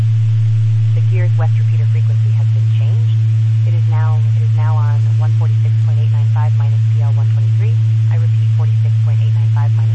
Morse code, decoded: none

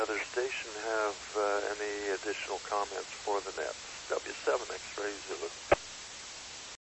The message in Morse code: none

